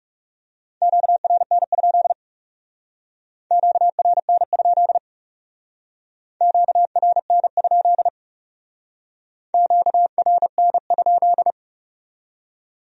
Morse code, heard as a keyed tone, about 39 wpm, 700 Hz. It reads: QRN? QRN? QRN? QRN?